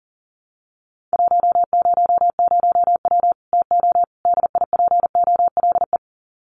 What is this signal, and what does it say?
Morse code, heard as a keyed tone, about 40 wpm, 700 Hz. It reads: 100W TO DIPOLE
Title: Common QSO Phrases 40wpm